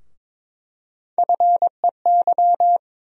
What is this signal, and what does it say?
Morse code, heard as a keyed tone, about 22 wpm, 700 Hz. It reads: FEY
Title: USA Call Sign Postfixes - 3 Letters 22wpm